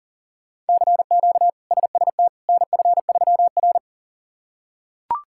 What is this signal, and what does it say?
Morse code, heard as a keyed tone, about 40 wpm, 700 Hz. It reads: CQ SST NF3R
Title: K1USN SST QSO Practice - 40wpm